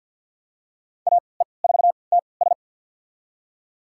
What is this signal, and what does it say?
Morse code, heard as a keyed tone, about 51 wpm, spaced as for 28 wpm, 700 Hz. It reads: AE4TS
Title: USA Call Signs - 3x Character Spacing - 50wpm